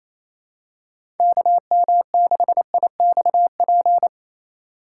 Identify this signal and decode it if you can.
Morse code, heard as a keyed tone, about 27 wpm, 700 Hz. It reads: KM6IXP
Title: USA Call Signs - 6 Characters - 2x Character Spacing - 28wpm